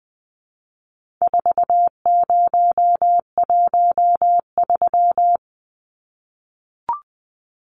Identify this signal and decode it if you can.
Morse code, heard as a keyed tone, about 20 wpm, 700 Hz. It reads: 4013
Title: Numbers - 4-Digit 20wpm